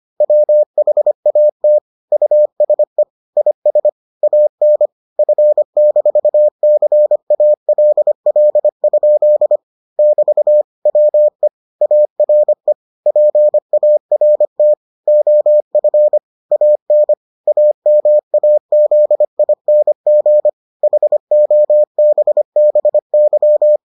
Morse code, heard as a keyed tone, about 25 wpm, 600 Hz. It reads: WHAT USE IS AN F-CALL? <BT> WE ARE PART OF AN AMAZING HOBBY